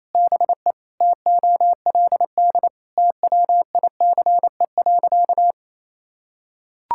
Morse code, heard as a keyed tone, about 28 wpm, 700 Hz. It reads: BE TOLD TWICE.
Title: Sentences from Top 1600 Words 28wpm